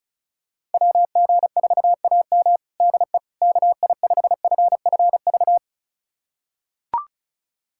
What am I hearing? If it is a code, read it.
Morse code, heard as a keyed tone, about 36 wpm, 700 Hz. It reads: WG4AM DE KI5FFV
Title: USA Callsign DE Callsign 35wpm